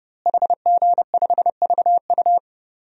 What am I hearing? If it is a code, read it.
Morse code, heard as a keyed tone, about 30 wpm, 700 Hz. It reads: HG5VU
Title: DXCC-100 Call Signs - 5 Characters - Repeated 3x - 30wpm